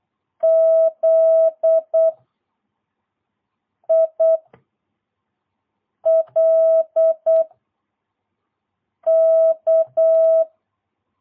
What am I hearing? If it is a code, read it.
Morse code, heard as a keyed tone, about 8 wpm, 650 Hz. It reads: Z I L K
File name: jh8cbh / 2022cw